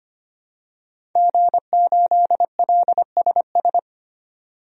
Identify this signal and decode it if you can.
Morse code, heard as a keyed tone, about 25 wpm, 700 Hz. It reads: G8LSS
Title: DXCC-100 Call Signs - 5 Characters - 25wpm